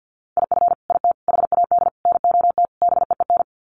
Morse code, heard as a keyed tone, about 50 wpm, 700 Hz. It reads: IF IT HAD NOT BEEN
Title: Sets of 5 Words Encore 50wpm